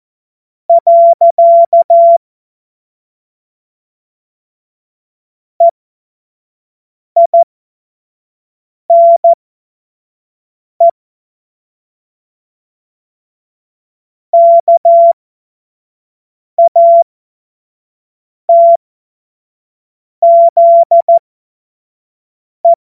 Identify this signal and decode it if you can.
Morse code, heard as a keyed tone, about 14 wpm, spaced as for 5 wpm, 680 Hz. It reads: . EINE KATZE
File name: Morsekurs / Witze / Witz 05. April 2022 25 F70